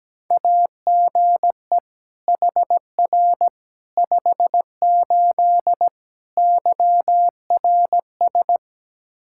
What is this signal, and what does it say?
Morse code, heard as a keyed tone, about 17 wpm, 700 Hz. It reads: AGE HR 58 YRS
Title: Practice QSO - Exchange 3 - More Info - 17wpm